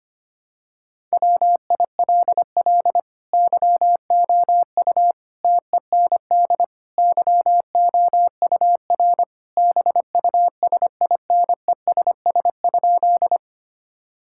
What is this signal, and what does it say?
Morse code, heard as a keyed tone, about 25 wpm, 700 Hz. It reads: WILL YOU TEND YOUR BUSINESS?
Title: Sentences from Top 1000 Words - Review Part 5 - 25wpm